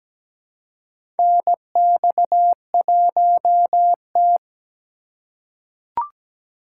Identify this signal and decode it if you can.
Morse code, heard as a keyed tone, about 17 wpm, 700 Hz. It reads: NX1T
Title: USA Call Signs - 4 Characters - 2x Character Spacing - 17wpm